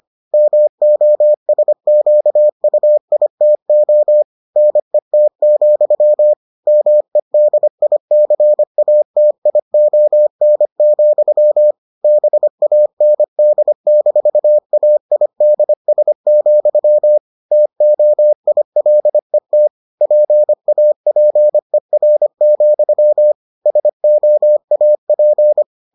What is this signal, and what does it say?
Morse code, heard as a keyed tone, about 25 wpm, 600 Hz. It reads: MOSQUITO NET, MEDICATION, BAND-AIDS, TOILET PAPER, SOAP